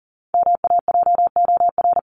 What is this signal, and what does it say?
Morse code, heard as a keyed tone, about 40 wpm, 700 Hz. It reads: MAJOR
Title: Top 501-750 Words 40wpm